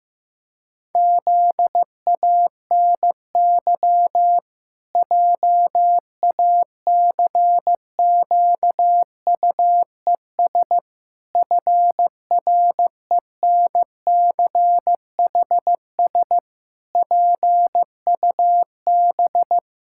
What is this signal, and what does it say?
Morse code, heard as a keyed tone, about 15 wpm, 700 Hz. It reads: ZANY JACQUES FRENCHS PUB
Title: Pangram Challenge v2 - 15wpm